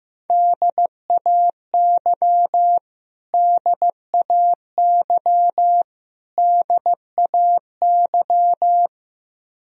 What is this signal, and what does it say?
Morse code, heard as a keyed tone, about 15 wpm, 700 Hz. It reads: DAY DAY DAY
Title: Top 100 Words Repeated 15wpm